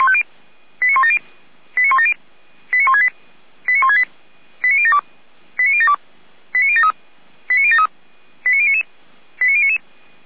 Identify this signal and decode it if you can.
Five-tone selective call (ZVEI), heard as quick runs of five tones, tones of about 70 ms. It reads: 89170 89170 89168 89168 89072 89072 89073 89073 89099 89099
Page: ZVEI